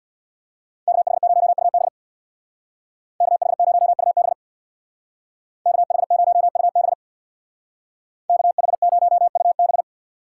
Morse code, heard as a keyed tone, about 64 wpm, 700 Hz. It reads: KH0UB KH0UB KH0UB KH0UB